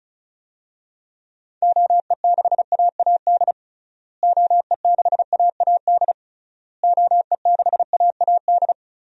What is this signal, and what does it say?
Morse code, heard as a keyed tone, about 36 wpm, 700 Hz. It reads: OE6AAD OE6AAD OE6AAD
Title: DXCC-100 Call Signs - 6 Characters - Repeated 3x - 35wpm